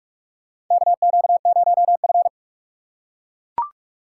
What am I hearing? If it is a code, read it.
Morse code, heard as a keyed tone, about 41 wpm, 700 Hz. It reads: KQ0F
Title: USA Call Signs - 4 Characters - 4x Character Spacing - 45wpm